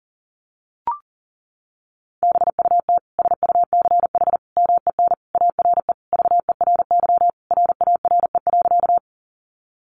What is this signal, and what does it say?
Morse code, heard as a keyed tone, about 40 wpm, 700 Hz. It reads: BUT SUCH MEN ARE VERY RARE.
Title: Sentences from Top 1100 Words 40wpm